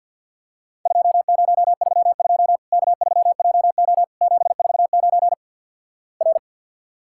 Morse code, heard as a keyed tone, about 50 wpm, 700 Hz. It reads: 2032 K3JO 749 R
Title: Worked All Europe DX Contest - QTC Exchange - 50wpm